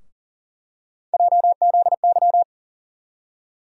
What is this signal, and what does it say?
Morse code, heard as a keyed tone, about 40 wpm, 700 Hz. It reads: JZY
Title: USA Call Sign Postfixes - 3 Letters 40wpm